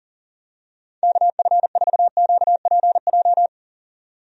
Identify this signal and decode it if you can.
Morse code, heard as a keyed tone, about 38 wpm, 700 Hz. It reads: KF4QPJ
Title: USA Call Signs - 6 Characters - 40wpm